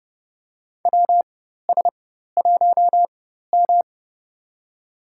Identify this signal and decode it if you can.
Morse code, heard as a keyed tone, about 30 wpm, spaced as for 14 wpm, 700 Hz. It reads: WS1M
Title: USA Call Signs - 4 Characters - 4x Character Spacing - 30wpm